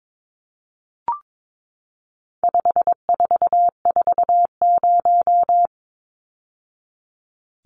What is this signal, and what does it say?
Morse code, heard as a keyed tone, about 22 wpm, 700 Hz. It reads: 5440
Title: Numbers - 4-Digit 22wpm